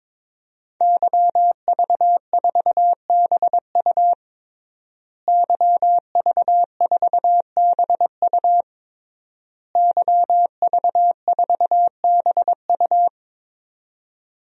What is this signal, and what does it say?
Morse code, heard as a keyed tone, about 22 wpm, 700 Hz. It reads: YV4BU YV4BU YV4BU
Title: DXCC-100 Call Signs - 5 Characters - Repeated 3x - 22wpm